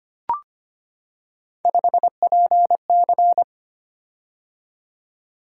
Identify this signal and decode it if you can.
Morse code, heard as a keyed tone, about 25 wpm, 700 Hz. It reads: 5PC
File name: Three Letter-Number Combinations - 25wpm